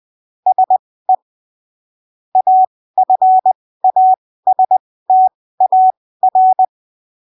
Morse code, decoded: SE AFASTAR